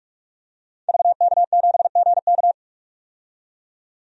Morse code, heard as a keyed tone, about 45 wpm, 700 Hz. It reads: VK7GK
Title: DXCC-100 Call Signs - 5 Characters - 45wpm